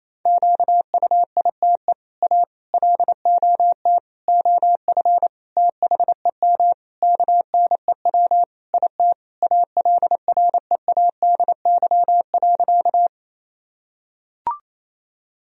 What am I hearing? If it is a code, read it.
Morse code, heard as a keyed tone, about 28 wpm, 700 Hz. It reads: QUITE A LOT OF THEM KNEW IT ALREADY.
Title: Sentences from Top 1000 Words - Review Part 3 - 28wpm